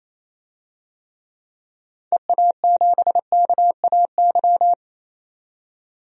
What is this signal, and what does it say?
Morse code, heard as a keyed tone, about 28 wpm, 700 Hz. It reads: EA7KAY
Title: DXCC-100 Call Signs - 6 Characters - 28wpm